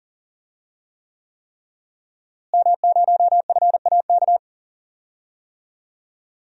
Morse code, heard as a keyed tone, about 40 wpm, 700 Hz. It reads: M0FAK
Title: DXCC-100 Call Signs - 5 Characters - Repeated 3x - 40wpm